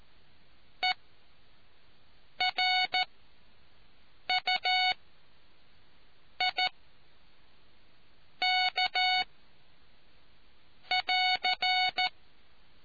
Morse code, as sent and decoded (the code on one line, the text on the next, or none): . .-. ..- .. -.- .-.-.
ERUIK<AR>